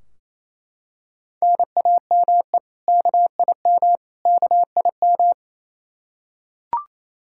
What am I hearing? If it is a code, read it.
Morse code, heard as a keyed tone, about 27 wpm, 700 Hz. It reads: NAME KIM KIM
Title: K1USN SST QSO Practice 28wpm